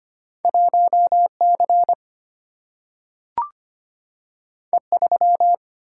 Morse code, heard as a keyed tone, about 25 wpm, 700 Hz. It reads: 1C E3
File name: Two Letter-Number Combinations - 25wpm